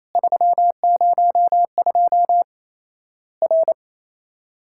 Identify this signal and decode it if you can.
Morse code, heard as a keyed tone, about 28 wpm, 700 Hz. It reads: 302 R
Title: Worked All Europe DX Contest - QTC Exchange - 28wpm